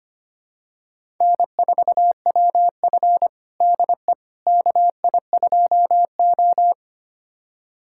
Morse code, heard as a keyed tone, about 25 wpm, 700 Hz. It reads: N4WF DE KI2O
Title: USA Callsign DE Callsign - 4 Characters - 25wpm